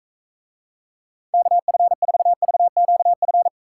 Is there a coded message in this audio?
Yes: Morse code from 1.3 to 3.5 s, heard as a keyed tone, about 40 wpm, 700 Hz. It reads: KF4VQF